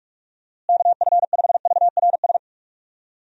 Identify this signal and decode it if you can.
Morse code, heard as a keyed tone, about 42 wpm, 700 Hz. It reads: KF5VRS
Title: USA Call Signs - 6 Characters - Repeated 2x - 45wpm